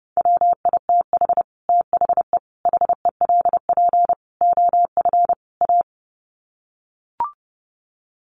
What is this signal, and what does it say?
Morse code, heard as a keyed tone, about 30 wpm, 700 Hz. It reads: WITH THE HELP OF A
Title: Sets of 5 Words 30wpm